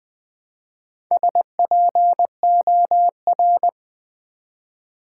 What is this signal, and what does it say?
Morse code, heard as a keyed tone, about 20 wpm, 700 Hz. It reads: SPOR
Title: Build A Word - 1 Letter At A Time - 3 to 5-Letter Words - 20wpm